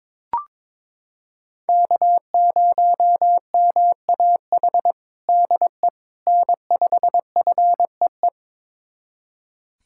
Morse code, heard as a keyed tone, about 22 wpm, 700 Hz. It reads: K0MAH DE N5FEE
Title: USA Callsign DE Callsign 22wpm